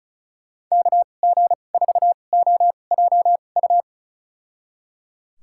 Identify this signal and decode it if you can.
Morse code, heard as a keyed tone, about 34 wpm, spaced as for 25 wpm, 700 Hz. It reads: KG4OJU
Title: USA Call Signs - 6 Characters - 2x Character Spacing - 35wpm